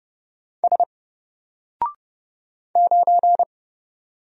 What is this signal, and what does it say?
Morse code, heard as a keyed tone, about 30 wpm, 700 Hz. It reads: S 9
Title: Single Letter-Number Alt 30wpm